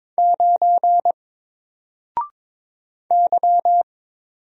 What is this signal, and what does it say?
Morse code, heard as a keyed tone, about 22 wpm, 700 Hz. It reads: 9 Y
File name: Single-Letter-Number-22wpm